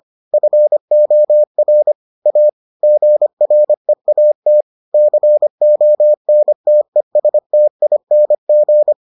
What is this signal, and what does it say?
Morse code, heard as a keyed tone, about 25 wpm, 600 Hz. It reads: FOR A GREAT CONTESTING